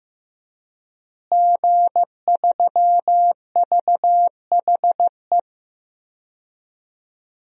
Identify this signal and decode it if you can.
Morse code, heard as a keyed tone, about 15 wpm, 700 Hz. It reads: G3VHE
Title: DXCC-100 Call Signs - 5 Characters - Repeated 2x - 15wpm